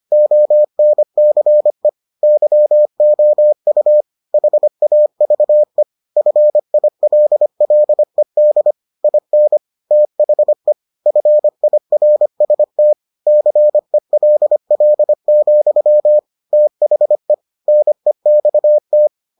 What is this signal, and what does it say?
Morse code, heard as a keyed tone, about 25 wpm, 600 Hz. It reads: ONCE YOU HAVE FILLED IN THE FIRST CELL, THE NEXT